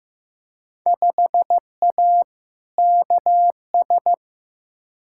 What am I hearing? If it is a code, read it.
Morse code, heard as a keyed tone, about 15 wpm, 700 Hz. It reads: 5A KS
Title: Field Day Exchanges Encore - 15wpm